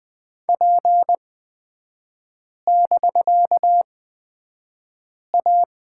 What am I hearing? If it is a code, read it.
Morse code, heard as a keyed tone, about 20 wpm, spaced as for 5 wpm, 700 Hz. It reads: P<BK>A
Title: 250 Lesson 40e - BK - Three Characters 20wpm